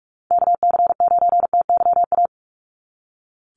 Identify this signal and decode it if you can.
Morse code, heard as a keyed tone, about 44 wpm, 700 Hz. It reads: KC9TYA
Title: NA SOTA Chaser Callsigns 45wpm